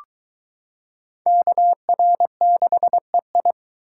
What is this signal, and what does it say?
Morse code, heard as a keyed tone, about 23 wpm, 700 Hz. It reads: KR6EI